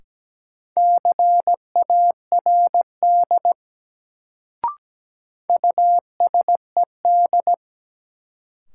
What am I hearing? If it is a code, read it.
Morse code, heard as a keyed tone, about 17 wpm, 700 Hz. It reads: CARD USED
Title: Sentences from Top 600 Words - New Word Drill - 17wpm